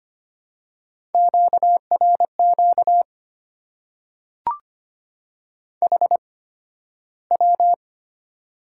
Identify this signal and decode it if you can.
Morse code, heard as a keyed tone, about 25 wpm, spaced as for 13 wpm, 700 Hz. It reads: QRQ HW
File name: Top 90 QSO Elements - 8x Character Spacing - 25wpm